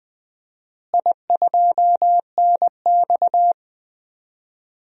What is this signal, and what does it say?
Morse code, heard as a keyed tone, about 20 wpm, 700 Hz. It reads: I2NX
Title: DXCC-100 Call Signs - 4 Characters - Repeated 2x - 20wpm